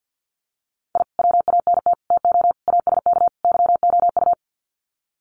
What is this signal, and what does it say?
Morse code, heard as a keyed tone, about 50 wpm, 700 Hz. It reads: I WANT TO ASK YOU
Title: Sets of 5 Words 50wpm